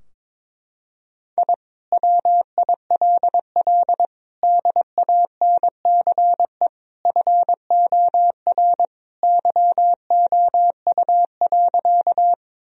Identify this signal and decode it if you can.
Morse code, heard as a keyed tone, about 22 wpm, 700 Hz. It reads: I WILL DANCE FOR YOU.
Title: Sentences from Top 1000 Words - Review Part 4 - 22wpm